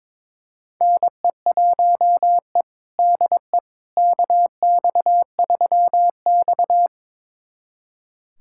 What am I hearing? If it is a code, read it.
Morse code, heard as a keyed tone, about 22 wpm, 700 Hz. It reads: NE1E DE KX3X
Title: USA Callsign DE Callsign - 4 Characters - 22wpm